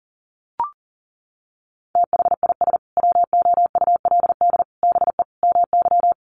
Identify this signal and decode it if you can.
Morse code, heard as a keyed tone, about 37 wpm, 700 Hz. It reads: THIS WOULD BE MY
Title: Sentences from Top 1200 Words 40wpm